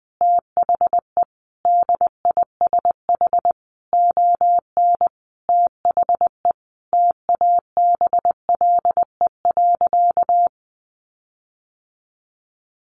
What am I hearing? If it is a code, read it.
Morse code, heard as a keyed tone, about 20 wpm, 700 Hz. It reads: THE DISH ON THE TABLE.
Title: Sentences from Top 1300 Words 20wpm